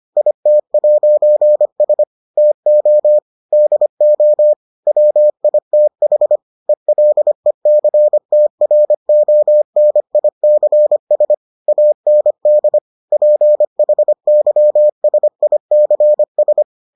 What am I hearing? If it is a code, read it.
Morse code, heard as a keyed tone, about 25 wpm, 600 Hz. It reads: IT'S TO DO WITH ELECTRONICS AND PHYSICS